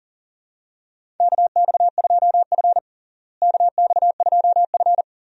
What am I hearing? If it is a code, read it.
Morse code, heard as a keyed tone, about 37 wpm, 700 Hz. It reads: KX2F KX2F
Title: USA Call Signs - 4 Characters - Repeated 2x - 40wpm